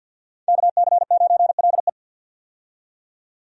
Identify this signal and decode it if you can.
Morse code, heard as a keyed tone, about 51 wpm, 700 Hz. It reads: KC9LE
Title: USA Call Signs - 5 Characters - 50wpm